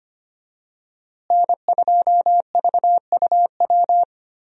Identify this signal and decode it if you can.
Morse code, heard as a keyed tone, about 25 wpm, 700 Hz. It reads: N2VUW
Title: USA Call Signs - 5 Characters - Repeated 3x - 25wpm